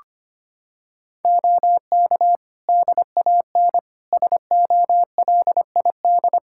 Morse code, heard as a keyed tone, about 25 wpm, 700 Hz. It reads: OK DAN SOLID
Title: Practice QSO - Exchange 3 - More Info - 25wpm